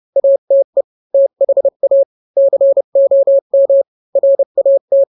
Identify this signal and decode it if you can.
Morse code, heard as a keyed tone, about 30 wpm, 550 Hz. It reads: ATE THA COM RAT
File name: n-grams / 3-grams 30 25 WPM